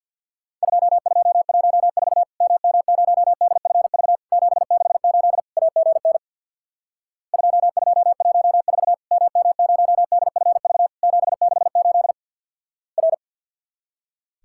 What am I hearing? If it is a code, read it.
Morse code, heard as a keyed tone, about 50 wpm, 700 Hz. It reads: 2214 MM0DFV 768 AGN 2214 MM0DFV 768 R
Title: Worked All Europe DX Contest - QTC Exchange with Repeats - 50wpm